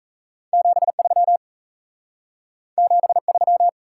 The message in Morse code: 73 73

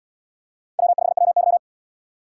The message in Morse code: W5WY

